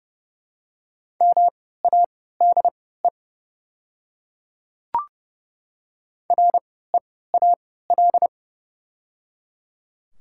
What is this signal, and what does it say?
Morse code, heard as a keyed tone, about 30 wpm, spaced as for 17 wpm, 700 Hz. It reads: MADE REAL
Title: 3x Character Spacing - Top 300 Words - 30wpm